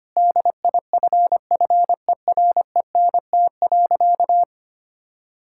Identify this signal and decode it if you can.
Morse code, heard as a keyed tone, about 25 wpm, 700 Hz. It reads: DIFFERENT.
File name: practice / 25wpm / Sentences with 4-Word Phrases - 25wpm